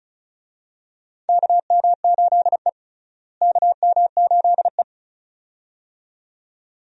Morse code, heard as a keyed tone, about 36 wpm, 700 Hz. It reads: KM8E KM8E
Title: USA Call Signs - 4 Characters - Repeated 2x - 35wpm